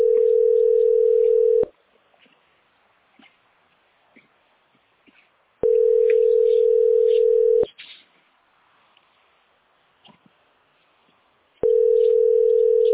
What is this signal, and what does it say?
Ringback tone (North America), 440 + 480 Hz.